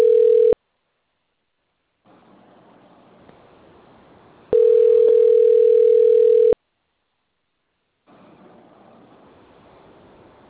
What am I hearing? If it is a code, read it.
Ringback tone (North America), 440 + 480 Hz.